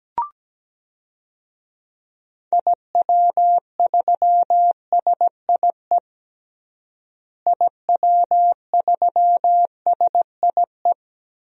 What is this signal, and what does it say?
Morse code, heard as a keyed tone, about 17 wpm, 700 Hz. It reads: IW3SIE IW3SIE
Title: DXCC-100 Call Signs - 6 Characters - Repeated 3x - 17wpm